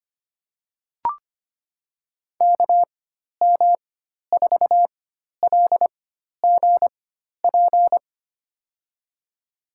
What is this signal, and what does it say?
Morse code, heard as a keyed tone, about 25 wpm, spaced as for 12 wpm, 700 Hz. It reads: KM4LGP
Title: USA Call Signs - 4x Character Spacing - 25wpm